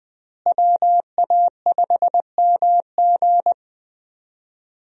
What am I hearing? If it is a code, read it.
Morse code, heard as a keyed tone, about 20 wpm, 700 Hz. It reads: WA5MG
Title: USA Call Signs - 5 Characters - Repeated 2x - 20wpm